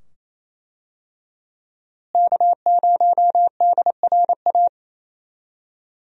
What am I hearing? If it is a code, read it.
Morse code, heard as a keyed tone, about 28 wpm, 700 Hz. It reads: K0DRA
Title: USA Call Signs - 5 Characters - Repeated 2x - 28wpm